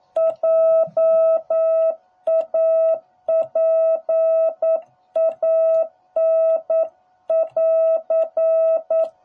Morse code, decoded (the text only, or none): JAPAN <AR>